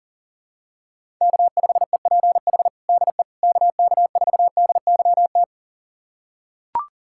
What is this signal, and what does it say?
Morse code, heard as a keyed tone, about 40 wpm, 700 Hz. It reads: K5EPH DE KK4DYT
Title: USA Callsign DE Callsign 40wpm